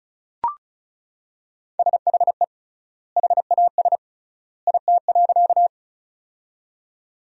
Morse code, heard as a keyed tone, about 36 wpm, 700 Hz. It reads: SHE HAS IT.
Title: 3x Word-Spacing - Sentences from Top 100 Words - 35wpm